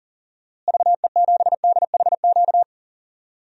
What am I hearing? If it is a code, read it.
Morse code, heard as a keyed tone, about 40 wpm, 700 Hz. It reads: VE7DHQ
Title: DXCC-100 Call Signs - 6 Characters - Repeated 3x - 40wpm